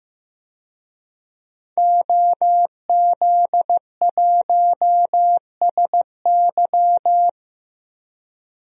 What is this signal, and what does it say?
Morse code, heard as a keyed tone, about 16 wpm, 700 Hz. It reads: OZ1SY
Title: DXCC-100 Call Signs - 5 Characters - Repeated 2x - 15wpm